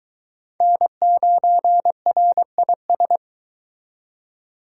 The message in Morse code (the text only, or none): N9RIS